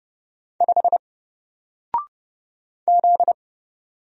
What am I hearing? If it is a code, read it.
Morse code, heard as a keyed tone, about 29 wpm, 700 Hz. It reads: 5 Z
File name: Single-Letter-Number-30wpm